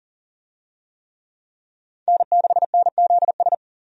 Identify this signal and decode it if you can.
Morse code, heard as a keyed tone, about 40 wpm, 700 Hz. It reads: N6NZS
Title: USA Call Signs - 5 Characters - 40wpm